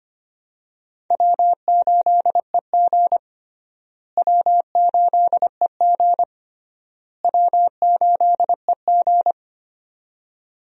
Morse code, heard as a keyed tone, about 25 wpm, 700 Hz. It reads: W8EG W8EG W8EG
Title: USA Call Signs - 4 Characters - Repeated 3x - 25wpm